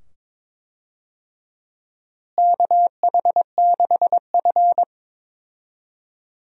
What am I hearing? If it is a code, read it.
Morse code, heard as a keyed tone, about 22 wpm, 700 Hz. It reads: KH6F
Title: DXCC-100 Call Signs - 4 Characters - Repeated 2x - 22wpm